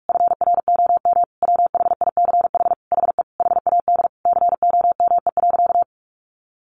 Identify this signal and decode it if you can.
Morse code, heard as a keyed tone, about 46 wpm, 700 Hz. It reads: FROM WHICH HE HAD COME.
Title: Sentences from Top 1500 Words 45wpm